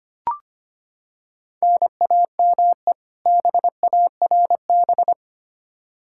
Morse code, heard as a keyed tone, about 25 wpm, 700 Hz. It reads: NAME BARB
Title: K1USN SST QSO Practice 25wpm